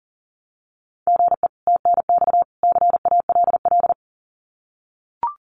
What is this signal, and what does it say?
Morse code, heard as a keyed tone, about 40 wpm, 700 Hz. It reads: GE TNX CALL
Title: Common QSO Phrases 40wpm